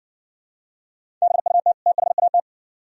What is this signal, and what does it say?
Morse code, heard as a keyed tone, about 60 wpm, 700 Hz. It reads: BUT THAT